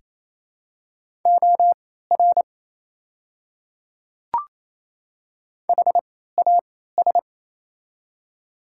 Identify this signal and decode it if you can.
Morse code, heard as a keyed tone, about 28 wpm, spaced as for 16 wpm, 700 Hz. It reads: OR HAS
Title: 3x Character Spacing - Top 200 Words - 28wpm